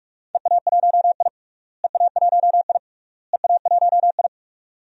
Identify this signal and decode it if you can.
Morse code, heard as a keyed tone, about 43 wpm, 700 Hz. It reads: EA1I EA1I EA1I
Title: DXCC-100 Call Signs - 4 Characters - Repeated 3x - 45wpm